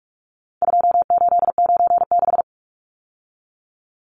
Morse code, heard as a keyed tone, about 44 wpm, 700 Hz. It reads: 2896
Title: Numbers - 4-Digit 45wpm